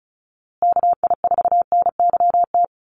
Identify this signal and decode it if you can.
Morse code, heard as a keyed tone, about 34 wpm, 700 Hz. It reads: KI4NYT
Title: USA Call Signs Repeated 2x 35wpm